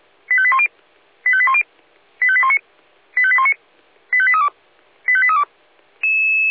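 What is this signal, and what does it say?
Five-tone selective call (ZVEI), heard as quick runs of five tones, tones of about 70 ms. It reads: 85810 85810 85819 85819 85832 85832
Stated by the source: Five-tone ZVEI selective call received on an SDR.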